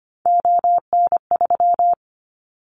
Morse code, heard as a keyed tone, about 25 wpm, 700 Hz. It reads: ON3
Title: Callsign Prefixes for Top 25 DXCC - 25wpm